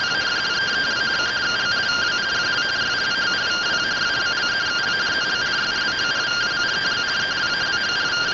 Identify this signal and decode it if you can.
Two-tone teleprinter signal, about 170 Hz shift.